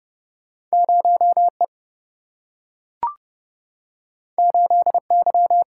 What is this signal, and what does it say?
Morse code, heard as a keyed tone, about 30 wpm, 700 Hz. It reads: 0E 8Y
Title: Single-Letter Single-Number 30wpm